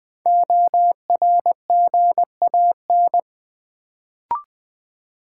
Morse code, heard as a keyed tone, about 20 wpm, 700 Hz. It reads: ORGAN
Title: Build A Word - 1 Letter At A Time - 3 to 5-Letter Words - 20wpm